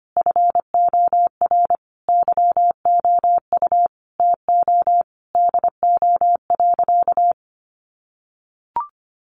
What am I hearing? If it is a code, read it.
Morse code, heard as a keyed tone, about 25 wpm, 700 Hz. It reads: FOR YOU TO DO.
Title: Sentences from Top 1300 Words 25wpm